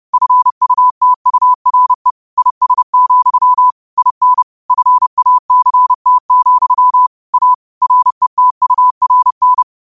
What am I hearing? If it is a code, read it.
Morse code, heard as a keyed tone, about 30 wpm, 1000 Hz. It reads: FUTURE IS, IN FACT, A RETURN